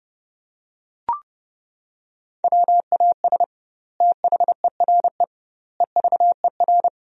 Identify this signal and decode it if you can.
Morse code, heard as a keyed tone, about 30 wpm, 700 Hz. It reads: WAS THERE EVER
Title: 2x Word-Spacing - Sentences from Top 400 Words - 30wpm